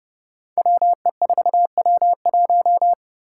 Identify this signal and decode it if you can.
Morse code, heard as a keyed tone, about 30 wpm, 700 Hz. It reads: WE4W1
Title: Five Challenging Letter-Number Combinations 30wpm